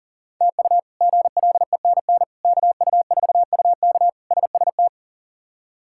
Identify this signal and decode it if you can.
Morse code, heard as a keyed tone, about 37 wpm, 700 Hz. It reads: TU GLENN KU4UK SST
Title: K1USN SST QSO Practice - 40wpm